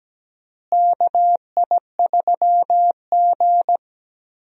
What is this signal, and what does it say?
Morse code, heard as a keyed tone, about 17 wpm, 700 Hz. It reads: KI3G
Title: USA Call Signs - 4 Characters - Repeated 3x - 17wpm